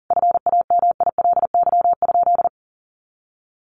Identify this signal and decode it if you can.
Morse code, heard as a keyed tone, about 40 wpm, 700 Hz. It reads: FAMILY?
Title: Sentences from Top 1400 Words 40wpm